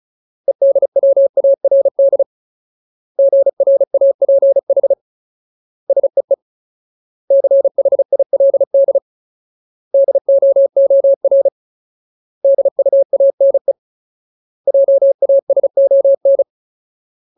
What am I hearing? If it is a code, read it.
Morse code, heard as a keyed tone, about 34 wpm, 550 Hz. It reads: EDWARD GRAPH SEE CHILD DOOR DUANE JASON
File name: Ultimate Word List / 35WPM Ultimate